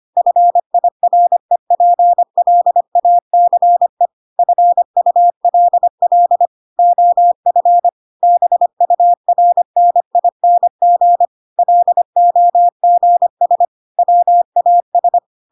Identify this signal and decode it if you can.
Morse code, heard as a keyed tone, about 25 wpm, 700 Hz. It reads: FIREPLACE FULL OF BURNING LOGS WAS
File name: Pinocchio / Pinocchio ch3 25 25 ik1hge